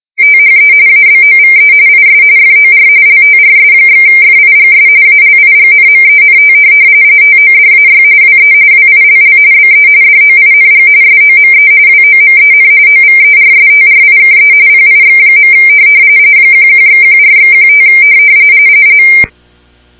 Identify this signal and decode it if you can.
RTTY, about 45.45 baud. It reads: THE ABOVE IS FOR LTRS/FIGS ALIGNMENT TODAY WE WILL BE DOING SOME WOMEN OVERSTRIKE PIX. 11/14/15